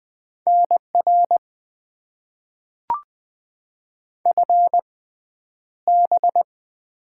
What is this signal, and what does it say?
Morse code, heard as a keyed tone, about 20 wpm, spaced as for 10 wpm, 700 Hz. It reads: NR FB
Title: Top 90 QSO Elements - 6x Character Spacing - 20wpm